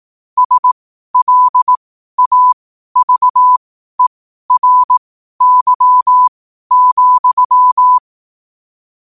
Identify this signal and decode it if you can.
Morse code, heard as a keyed tone, about 18 wpm, spaced as for 12 wpm, 1000 Hz. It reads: SLAVERY,
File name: qotdfiles / 12wpm